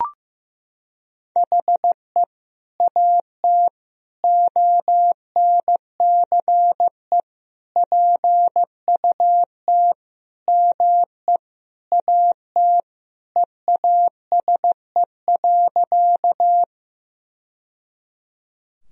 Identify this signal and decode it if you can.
Morse code, heard as a keyed tone, about 15 wpm, 700 Hz. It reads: HE AT ONCE PUT ME AT EASE.
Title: Sentences from Top 1000 Words - Review Part 1 - 15wpm